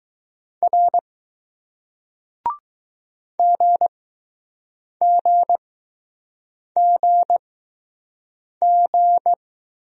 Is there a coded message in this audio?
Yes: Morse code from 0.6 to 9.3 s, heard as a keyed tone, about 20 wpm, spaced as for 6 wpm, 700 Hz. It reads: R GGGG